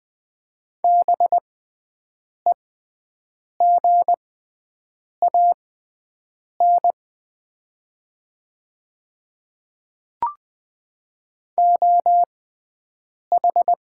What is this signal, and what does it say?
Morse code, heard as a keyed tone, about 20 wpm, spaced as for 7 wpm, 700 Hz. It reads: BEGAN OH
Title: 6x Character Spacing - Top 500 Words - 20wpm